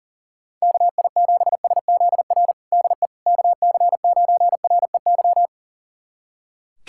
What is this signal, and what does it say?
Morse code, heard as a keyed tone, about 40 wpm, 700 Hz. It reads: KI7SZR DE KC9REY
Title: USA Callsign DE Callsign - 6 Characters - 40wpm